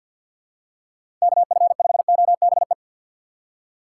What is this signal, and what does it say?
Morse code, heard as a keyed tone, about 46 wpm, 700 Hz. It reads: KF5OBE
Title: USA Call Signs - 6 Characters - 4x Character Spacing - 50wpm